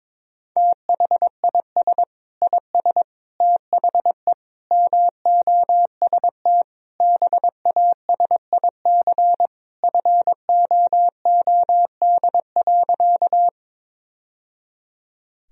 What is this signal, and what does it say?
Morse code, heard as a keyed tone, about 22 wpm, 700 Hz. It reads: THIS IS THE MOST BASIC FOOD.
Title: Sentences from Top 1000 Words - Review Part 3 - 22wpm